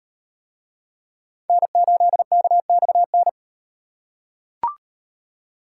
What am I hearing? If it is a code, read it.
Morse code, heard as a keyed tone, about 38 wpm, 700 Hz. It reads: N8KXN